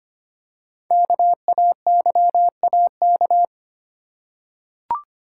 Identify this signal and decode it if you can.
Morse code, heard as a keyed tone, about 25 wpm, 700 Hz. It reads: KAYAK
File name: practice / 25wpm / Easy Words with Hints - 25wpm